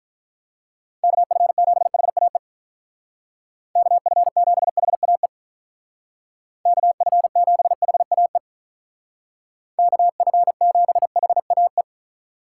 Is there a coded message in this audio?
Yes: Morse code from 1.0 to 11.8 s, heard as a keyed tone, about 45 wpm, 700 Hz. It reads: KF7HAE KF7HAE KF7HAE KF7HAE